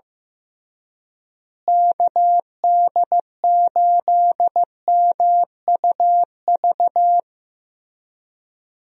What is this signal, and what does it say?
Morse code, heard as a keyed tone, about 15 wpm, 700 Hz. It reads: KD8MUV
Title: USA Call Signs - 6 Characters - Repeated 3x - 15wpm